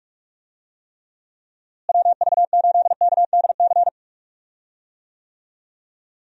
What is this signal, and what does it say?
Morse code, heard as a keyed tone, about 42 wpm, 700 Hz. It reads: WV8KDC
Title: USA Call Signs - 6 Characters - Repeated 2x - 45wpm